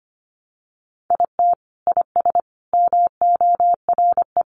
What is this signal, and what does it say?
Morse code, heard as a keyed tone, about 25 wpm, 700 Hz. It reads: IT IS MORE
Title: Sentences from Top 1200 Words 25wpm